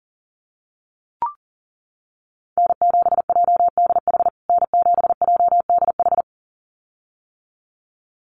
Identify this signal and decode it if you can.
Morse code, heard as a keyed tone, about 40 wpm, 700 Hz. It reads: N7JDH N7JDH
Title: USA Call Signs Repeated 2x 40wpm